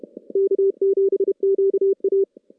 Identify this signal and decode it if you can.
Morse code, heard as a keyed tone, about 31 wpm, 390 Hz. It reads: K7QA
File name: contests / CW-SS-k5tr- -0000